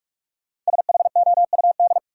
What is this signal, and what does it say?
Morse code, heard as a keyed tone, about 42 wpm, 700 Hz. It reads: SHOUD
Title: Sentences from Top 1000 Words - Review Part 4 - New Word Drill - 45wpm